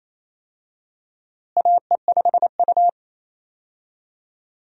Morse code, heard as a keyed tone, about 28 wpm, 700 Hz. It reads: AE5U
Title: USA Call Signs - 4 Characters - Repeated 2x - 28wpm